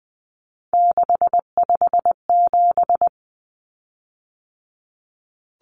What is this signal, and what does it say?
Morse code, heard as a keyed tone, about 20 wpm, 700 Hz. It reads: 657
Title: Common Cut Numbers - 3-Digit 20wpm